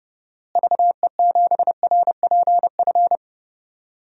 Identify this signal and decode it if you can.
Morse code, heard as a keyed tone, about 30 wpm, 700 Hz. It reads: VE7RPF
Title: DXCC-100 Call Signs - 6 Characters - Repeated 2x - 30wpm